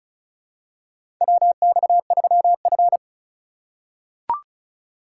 Morse code, heard as a keyed tone, about 36 wpm, 700 Hz. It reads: WX3F
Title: USA Call Signs - 4 Characters - 2x Character Spacing - 35wpm